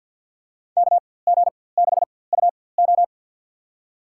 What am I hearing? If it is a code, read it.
Morse code, heard as a keyed tone, about 51 wpm, spaced as for 23 wpm, 700 Hz. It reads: KG6UO
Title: USA Call Signs - 5 Characters - 4x Character Spacing - 50wpm